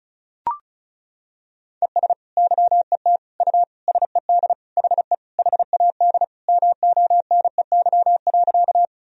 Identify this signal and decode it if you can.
Morse code, heard as a keyed tone, about 34 wpm, 700 Hz. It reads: ES YET U SED HE HAD MONEY.
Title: Sentences from Top 500 Words and Abbreviations 35wpm